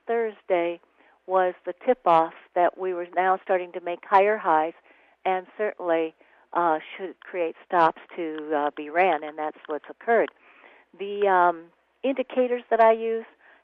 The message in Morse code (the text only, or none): none